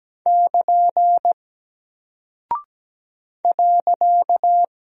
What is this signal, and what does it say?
Morse code, heard as a keyed tone, about 17 wpm, 700 Hz. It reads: <KN> .
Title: Single Character (Letter:Number:Punctuation:Prosign) Alt 17wpm